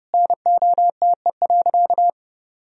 Morse code, heard as a keyed tone, about 30 wpm, 700 Hz. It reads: NOTE.
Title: Sentences from Top 1000 Words - Review Part 2 - 30wpm